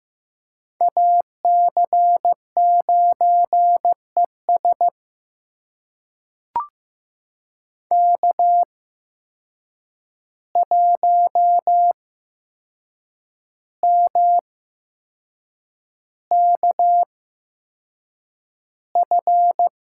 Morse code, heard as a keyed tone, about 15 wpm, spaced as for 6 wpm, 700 Hz. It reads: AC9ES K1MKF
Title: USA Call Signs - 5 Characters - 8x Character Spacing - 15wpm